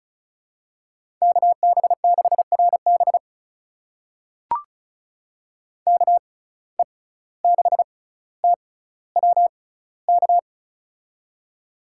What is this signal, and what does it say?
Morse code, heard as a keyed tone, about 36 wpm, spaced as for 17 wpm, 700 Hz. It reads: KB6RB KE6TWK
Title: USA Call Signs - 6x Character Spacing - 35wpm